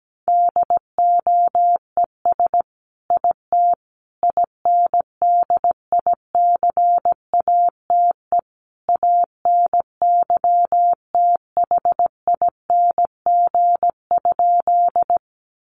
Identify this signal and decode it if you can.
Morse code, heard as a keyed tone, about 17 wpm, 700 Hz. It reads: DOES IT INDICATE ANYTHING?
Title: Sentences from Top 1300 Words 17wpm